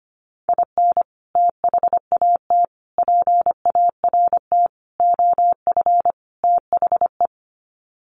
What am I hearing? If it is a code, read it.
Morse code, heard as a keyed tone, about 25 wpm, 700 Hz. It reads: IN THAT PART OF THE
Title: Sets of 5 Words Encore2 25wpm